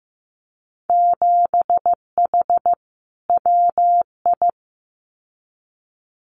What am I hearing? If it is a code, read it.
Morse code, heard as a keyed tone, about 15 wpm, 700 Hz. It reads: 7H WI
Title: Winter Field Day Exchanges Encore 15wpm